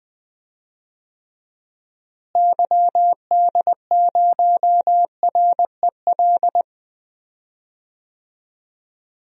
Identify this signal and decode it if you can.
Morse code, heard as a keyed tone, about 20 wpm, 700 Hz. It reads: YD0REL
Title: DXCC-100 Call Signs - 6 Characters - Repeated 2x - 20wpm